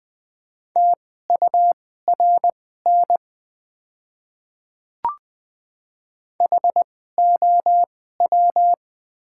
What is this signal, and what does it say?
Morse code, heard as a keyed tone, about 20 wpm, spaced as for 15 wpm, 700 Hz. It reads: TURN HOW
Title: 2x Character Spacing - Top 200 Words - 20wpm